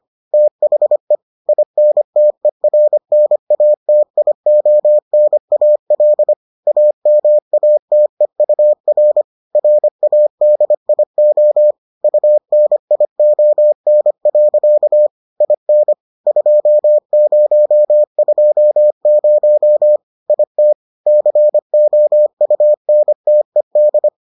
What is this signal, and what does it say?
Morse code, heard as a keyed tone, about 25 wpm, 600 Hz. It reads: THE INTERNATIONAL AMATEUR RADIO UNION. IN 2020 IT COUNTED